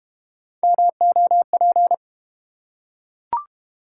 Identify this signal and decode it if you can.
Morse code, heard as a keyed tone, about 32 wpm, 700 Hz. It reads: MOP